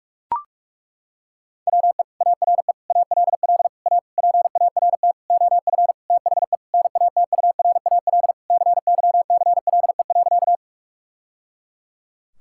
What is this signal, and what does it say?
Morse code, heard as a keyed tone, about 41 wpm, 700 Hz. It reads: WE ARE ALL A PART OF THE NATURAL CYCLE.
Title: Sentences from Top 1000 Words - Review Part 6 - 45wpm